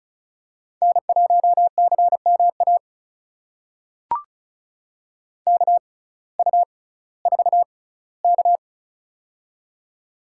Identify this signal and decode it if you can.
Morse code, heard as a keyed tone, about 36 wpm, spaced as for 19 wpm, 700 Hz. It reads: N1CMA KU4K
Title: USA Call Signs - 6x Character Spacing - 35wpm